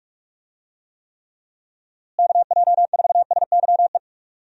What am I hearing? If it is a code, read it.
Morse code, heard as a keyed tone, about 46 wpm, 700 Hz. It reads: KJ4SYE
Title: USA Call Signs - 6 Characters - Repeated 2x - 45wpm